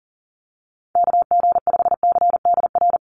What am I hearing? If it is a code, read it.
Morse code, heard as a keyed tone, about 40 wpm, 700 Hz. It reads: KG5CDR
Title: USA Call Signs 40wpm